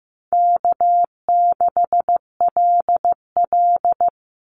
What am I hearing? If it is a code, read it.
Morse code, heard as a keyed tone, about 15 wpm, 700 Hz. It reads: K6LL
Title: Easy Call Signs Repeated 2x - 15wpm